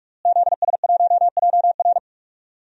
Morse code, heard as a keyed tone, about 46 wpm, 700 Hz. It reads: ZS1JR
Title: DXCC-100 Call Signs - 5 Characters - Repeated 3x - 45wpm